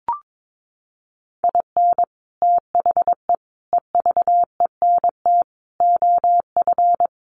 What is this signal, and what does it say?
Morse code, heard as a keyed tone, about 22 wpm, 700 Hz. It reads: IN THE EVENT OF
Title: Sets of 5 Words Encore 22wpm